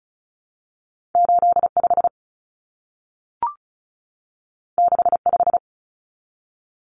Morse code, heard as a keyed tone, about 36 wpm, 700 Hz. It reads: 85 65
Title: Common Cut Numbers - 2-Digit 35wpm